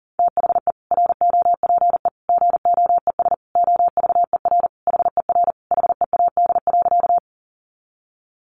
Morse code, heard as a keyed tone, about 40 wpm, 700 Hz. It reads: THE ROPE GOES OVER HER HEAD.
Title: Easy Phrases 40wpm